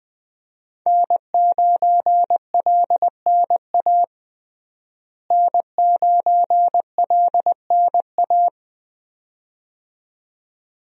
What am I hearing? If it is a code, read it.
Morse code, heard as a keyed tone, about 20 wpm, 700 Hz. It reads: N9LNA N9LNA
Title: USA Call Signs - 5 Characters - Repeated 2x - 20wpm